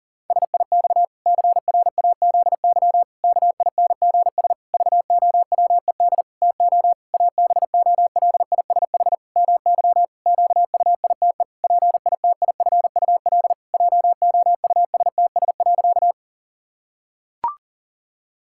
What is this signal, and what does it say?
Morse code, heard as a keyed tone, about 40 wpm, 700 Hz. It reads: SIX CRAZY KINGS VOWED TO ABOLISH MY QUITE PITIFUL JOUSTS.
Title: Pangram Challenge v2 - 40wpm